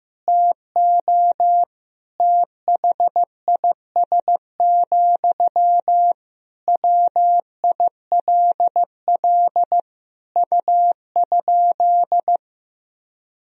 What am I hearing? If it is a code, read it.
Morse code, heard as a keyed tone, about 15 wpm, 700 Hz. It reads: TO THIS, WILL U?
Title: Sentences from Top 400 Words and Abbreviations 15wpm